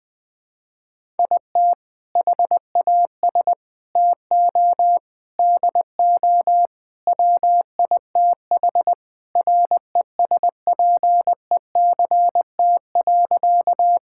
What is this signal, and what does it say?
Morse code, heard as a keyed tone, about 20 wpm, 700 Hz. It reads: IT HAS TO DO WITH RESPECT.
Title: Sentences with 4-Word Phrases - 20wpm